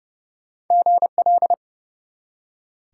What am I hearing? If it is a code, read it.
Morse code, heard as a keyed tone, about 30 wpm, 700 Hz. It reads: GL
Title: Top 90 QSO Elements 30wpm